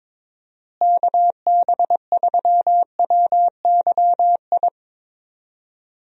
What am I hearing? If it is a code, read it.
Morse code, heard as a keyed tone, about 22 wpm, 700 Hz. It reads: KB3WYI
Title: USA Call Signs - 6 Characters - Repeated 3x - 22wpm